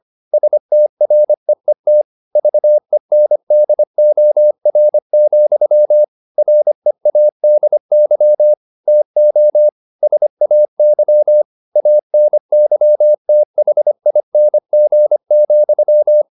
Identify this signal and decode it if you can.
Morse code, heard as a keyed tone, about 25 wpm, 600 Hz. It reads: STREET VENDOR, READY TO SAY ANYTHING,